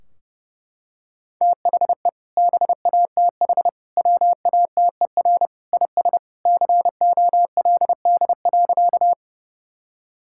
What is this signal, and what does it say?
Morse code, heard as a keyed tone, about 30 wpm, 700 Hz. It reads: THE BATH WATER IS COLD.
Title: Sentences from Top 1600 Words 30wpm